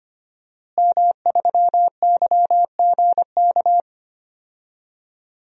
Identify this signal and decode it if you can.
Morse code, heard as a keyed tone, about 25 wpm, 700 Hz. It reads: M3YGK
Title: DXCC-100 Call Signs - 5 Characters - Repeated 2x - 25wpm